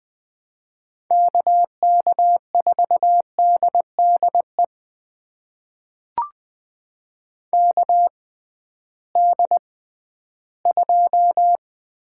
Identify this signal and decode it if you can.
Morse code, heard as a keyed tone, about 20 wpm, spaced as for 13 wpm, 700 Hz. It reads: KK4DDE KD2
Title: USA Call Signs - 6 Characters - 6x Character Spacing - 20wpm